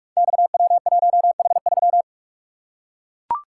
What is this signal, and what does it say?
Morse code, heard as a keyed tone, about 44 wpm, 700 Hz. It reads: XW1H3
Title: Code Groups - 5 Letter:Numbers - 45wpm